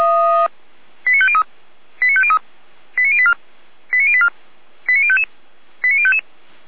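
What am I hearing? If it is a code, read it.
Five-tone selective call (ZVEI), heard as quick runs of five tones, tones of about 70 ms. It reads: 89503 89503 89084 89084 89055 89055
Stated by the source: ZVEI five-tone selective call received on an SDR.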